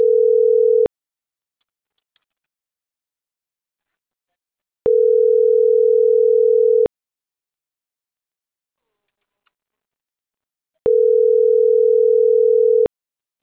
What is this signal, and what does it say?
Ringback tone (North America), 440 + 480 Hz.